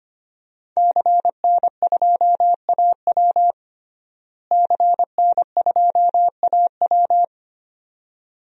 Morse code, heard as a keyed tone, about 25 wpm, 700 Hz. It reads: CN2AW CN2AW
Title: DXCC-100 Call Signs - 5 Characters - Repeated 2x - 25wpm